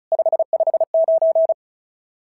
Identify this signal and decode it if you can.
Morse code, heard as a keyed tone, about 36 wpm, 690 Hz. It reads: 559
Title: SOTA QSO Practice 35wpm